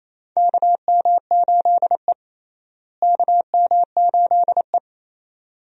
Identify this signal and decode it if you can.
Morse code, heard as a keyed tone, about 28 wpm, 700 Hz. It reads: KM8E KM8E
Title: USA Call Signs - 4 Characters - Repeated 2x - 28wpm